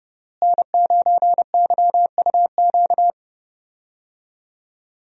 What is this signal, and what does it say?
Morse code, heard as a keyed tone, about 30 wpm, 700 Hz. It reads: N9YUQ
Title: USA Call Signs - 5 Characters - Repeated 2x - 30wpm